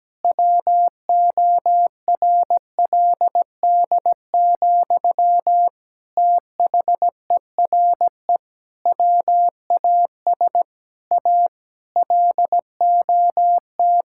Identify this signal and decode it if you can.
Morse code, heard as a keyed tone, about 17 wpm, 700 Hz. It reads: WORLD, THERE WAS A LOT
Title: Sentences from Top 1000 Words - Review Part 5 - 17wpm